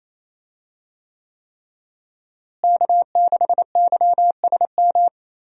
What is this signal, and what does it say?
Morse code, heard as a keyed tone, about 27 wpm, 700 Hz. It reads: K6YSM
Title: USA Call Signs - 5 Characters - Repeated 3x - 28wpm